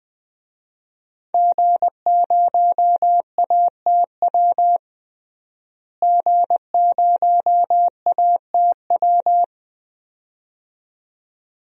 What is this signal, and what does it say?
Morse code, heard as a keyed tone, about 20 wpm, 700 Hz. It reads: G0ATW G0ATW
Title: DXCC-100 Call Signs - 5 Characters - Repeated 2x - 20wpm